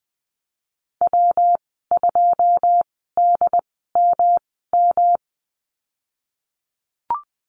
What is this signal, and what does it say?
Morse code, heard as a keyed tone, about 20 wpm, spaced as for 15 wpm, 700 Hz. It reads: W2DMM
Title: 128 Lesson 21g - Number 6 - Call Signs 20wpm